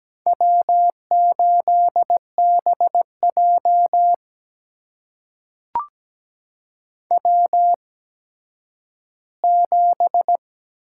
Morse code, heard as a keyed tone, about 17 wpm, spaced as for 10 wpm, 700 Hz. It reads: W8BJ W7
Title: USA Call Signs - 4 Characters - 8x Character Spacing - 17wpm